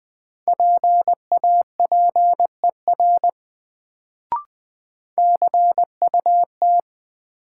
Morse code, heard as a keyed tone, about 21 wpm, 700 Hz. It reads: PAPER CUT
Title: Sentences from Top 300 Words - New Word Drill - 20wpm